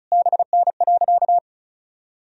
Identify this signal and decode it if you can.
Morse code, heard as a keyed tone, about 36 wpm, 700 Hz. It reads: BN.
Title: Sentences from Top 300 Words and Abbreviations 35wpm